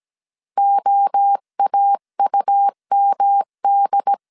Morse code, heard as a keyed tone, about 17 wpm, 800 Hz. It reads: OAUMD